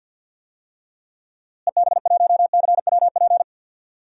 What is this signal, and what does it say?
Morse code, heard as a keyed tone, about 52 wpm, 700 Hz. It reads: EB1CWP
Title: DXCC-100 Call Signs - 6 Characters - Repeated 3x - 50wpm